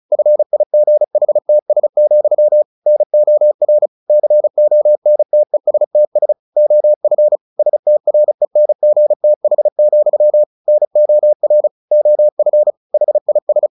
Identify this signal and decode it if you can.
Morse code, heard as a keyed tone, about 34 wpm, 600 Hz. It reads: FIGHTS, NOR CONTESTS OF STRENGTH, NOR OF HIS
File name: the old man and the sea / the old man and the sea 1 35wpm